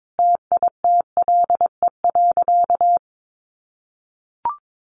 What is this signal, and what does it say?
Morse code, heard as a keyed tone, about 22 wpm, 700 Hz. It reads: TITLE.
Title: Sentences from Top 1200 Words 22wpm